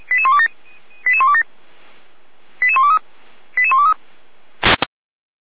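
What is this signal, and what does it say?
Five-tone selective call (ZVEI), heard as quick runs of five tones, tones of about 80 ms. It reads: 88127 88123 88123
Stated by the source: ZVEI five-tone selective call, code 88124, received with an SDR.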